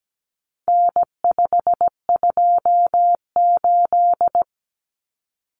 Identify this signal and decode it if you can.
Morse code, heard as a keyed tone, about 17 wpm, 700 Hz. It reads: N528
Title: Common Cut Numbers - 4-Digit 17wpm